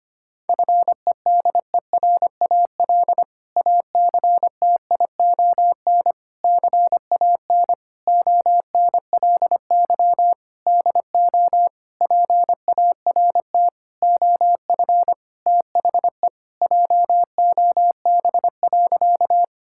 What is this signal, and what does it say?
Morse code, heard as a keyed tone, about 25 wpm, 700 Hz. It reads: FEDERAL ACTION CAN ONLY DO PART OF THE JOB.
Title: Sentences from Top 1000 Words - Review Part 5 - 25wpm